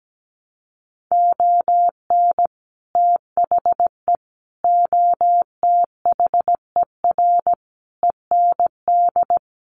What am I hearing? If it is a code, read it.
Morse code, heard as a keyed tone, about 17 wpm, 700 Hz. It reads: ON THE OTHER END
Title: Sets of 5 Words Encore 17wpm